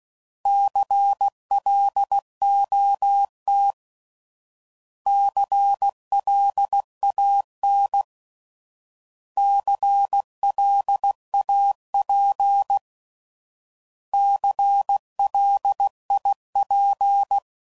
Morse code, heard as a keyed tone, about 16 wpm, 800 Hz. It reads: CLOT CLAN CLAP CLIP
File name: Prefixes / By First Letter / Cs